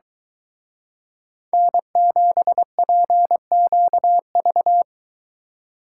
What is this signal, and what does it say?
Morse code, heard as a keyed tone, about 23 wpm, 700 Hz. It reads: N7PQV